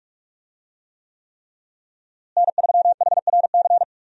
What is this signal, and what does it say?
Morse code, heard as a keyed tone, about 47 wpm, 700 Hz. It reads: N3HRC
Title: USA Call Signs - 5 Characters - Repeated 3x - 45wpm